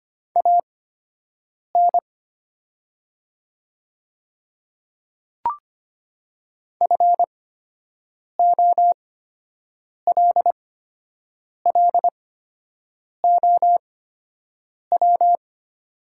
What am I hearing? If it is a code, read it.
Morse code, heard as a keyed tone, about 25 wpm, spaced as for 7 wpm, 700 Hz. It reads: AN FOLLOW
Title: 8x Character Spacing - Top 200 Words - 25wpm